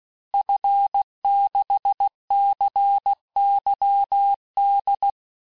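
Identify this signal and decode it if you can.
Morse code, heard as a keyed tone, about 16 wpm, 800 Hz. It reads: F6CYD
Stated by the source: Morse course text, end of QSO, sent at 15 words per minute.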